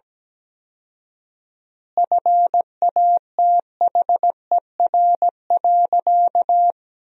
Morse code, heard as a keyed tone, about 17 wpm, 700 Hz. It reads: FATHER.
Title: 4x Word-Spacing - Sentences from Top 500 Words - 17wpm